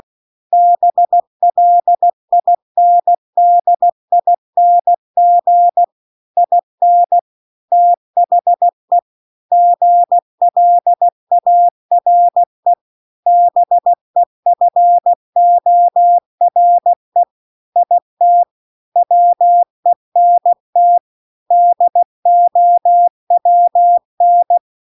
Morse code, heard as a keyed tone, about 16 wpm, 700 Hz. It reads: BLINDING IN THE GLARE BEFORE IT WENT DOWN